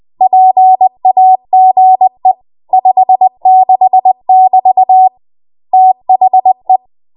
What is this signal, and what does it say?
Morse code, heard as a keyed tone, about 20 wpm, 750 Hz. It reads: PAGE 56 <BT> THE
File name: Morse / 20WPM